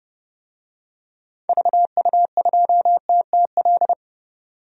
Morse code, heard as a keyed tone, about 30 wpm, 700 Hz. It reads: VU2TTL
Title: DXCC-100 Call Signs - 6 Characters - 30wpm